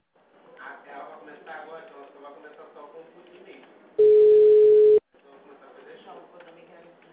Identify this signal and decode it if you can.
Ringback tone (Europe, 425 Hz), with speech.